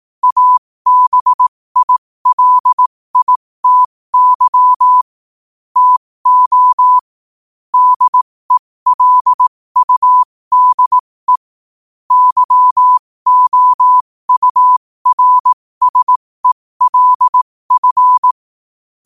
Morse code, heard as a keyed tone, about 18 wpm, spaced as for 15 wpm, 1000 Hz. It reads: ABILITY TO DELUDE YOURSELF
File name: qotdfiles / 15wpm